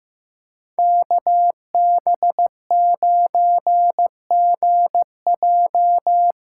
Morse code, heard as a keyed tone, about 15 wpm, 700 Hz. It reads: KB9GJ
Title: USA Callsign DE Callsign - 5 Characters - 15wpm